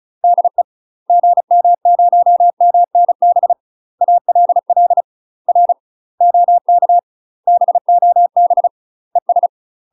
Morse code, heard as a keyed tone, about 36 wpm, 700 Hz. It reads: DE GM0MNB ALL R OK BOB ES
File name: UK MASTER QSOs TRAINING / 35WPM / UK MASTER QSOs 0002